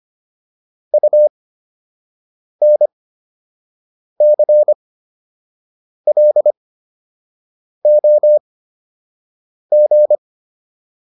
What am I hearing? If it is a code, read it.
Morse code, heard as a keyed tone, about 25 wpm, spaced as for 6 wpm, 600 Hz. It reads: UNCLOG